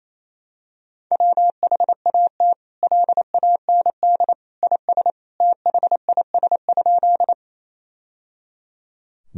Morse code, heard as a keyed tone, about 28 wpm, 700 Hz. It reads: WHAT LAND IS THIS?
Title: Sentences from Top 1000 Words - Review Part 1 - 28wpm